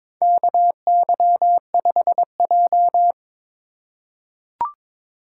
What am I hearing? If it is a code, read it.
Morse code, heard as a keyed tone, about 22 wpm, 700 Hz. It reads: KY5J
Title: USA Call Signs - 4 Characters - 2x Character Spacing - 22wpm